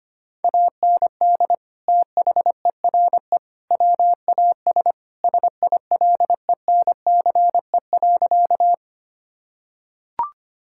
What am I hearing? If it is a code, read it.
Morse code, heard as a keyed tone, about 25 wpm, 700 Hz. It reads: AND THERE WAS SILENCE.
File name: practice / 25wpm / Sentences with 3-Word Phrases - 25wpm